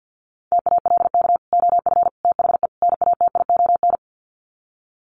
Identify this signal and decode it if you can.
Morse code, heard as a keyed tone, about 50 wpm, 700 Hz. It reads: TALK OF THE NATION
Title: Sets of 4 Words 50wpm